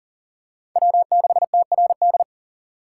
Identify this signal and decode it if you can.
Morse code, heard as a keyed tone, about 40 wpm, 700 Hz. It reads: W6TRD
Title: USA Call Signs - 5 Characters - 3x Character Spacing - 40wpm